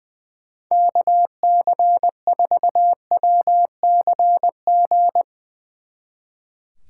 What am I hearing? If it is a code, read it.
Morse code, heard as a keyed tone, about 20 wpm, 700 Hz. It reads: KC4WCG
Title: Speed Racing - USA 2x3 Call Signs 20wpm